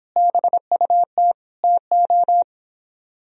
Morse code, heard as a keyed tone, about 29 wpm, 700 Hz. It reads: BUT TO